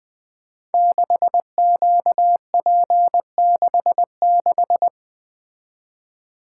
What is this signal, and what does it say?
Morse code, heard as a keyed tone, about 20 wpm, 700 Hz. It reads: 6QP66
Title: Code Groups - 5 Letter:Numbers - 20wpm